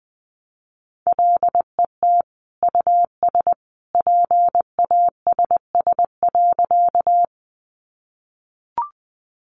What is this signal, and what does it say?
Morse code, heard as a keyed tone, about 19 wpm, 700 Hz. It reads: LET US PASS.
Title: Sentences from Top 400 Words 20wpm